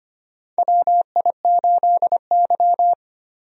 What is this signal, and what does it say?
Morse code, heard as a keyed tone, about 25 wpm, 700 Hz. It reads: WI8Y
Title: USA Call Signs - 4 Characters - Repeated 3x - 25wpm